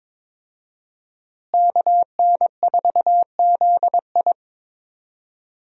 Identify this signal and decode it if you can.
Morse code, heard as a keyed tone, about 22 wpm, 700 Hz. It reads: KN4ZI
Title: USA Call Signs - 5 Characters - Repeated 2x - 22wpm